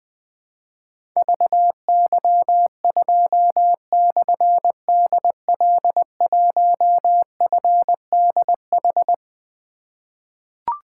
Morse code, heard as a keyed tone, about 20 wpm, 700 Hz. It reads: VY2/DL1FDH
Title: Difficult Call Signs 20wpm